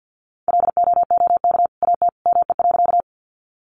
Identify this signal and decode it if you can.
Morse code, heard as a keyed tone, about 48 wpm, 700 Hz. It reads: LOOK AT ME.
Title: Sentences from Top 200 Words 50wpm